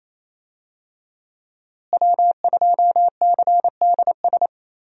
Morse code, heard as a keyed tone, about 27 wpm, 700 Hz. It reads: W2CDS
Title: USA Call Signs - 5 Characters - Repeated 3x - 28wpm